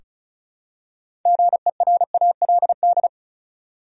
Morse code, heard as a keyed tone, about 36 wpm, 700 Hz. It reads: GERALD
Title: USA Common Names - 35wpm